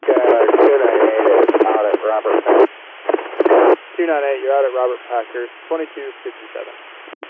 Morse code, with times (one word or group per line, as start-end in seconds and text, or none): none